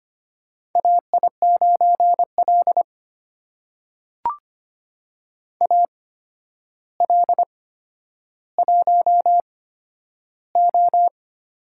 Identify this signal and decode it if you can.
Morse code, heard as a keyed tone, about 25 wpm, spaced as for 11 wpm, 700 Hz. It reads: AI9L AL1O
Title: USA Call Signs - 4 Characters - 8x Character Spacing - 25wpm